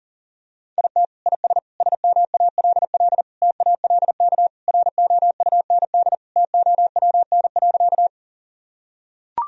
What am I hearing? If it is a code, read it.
Morse code, heard as a keyed tone, about 40 wpm, 700 Hz. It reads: IT IS SMALL TALK ROUND TOWN.
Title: Sentences from Top 1000 Words - Review Part 1 - 40wpm